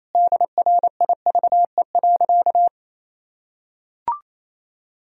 Morse code, heard as a keyed tone, about 28 wpm, 700 Hz. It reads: DRIVE.
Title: Sentences from Top 1000 Words - Review Part 2 - 28wpm